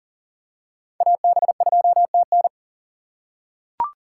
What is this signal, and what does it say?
Morse code, heard as a keyed tone, about 40 wpm, 700 Hz. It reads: AB2TN
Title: USA Call Signs - 6x Character Spacing - 40wpm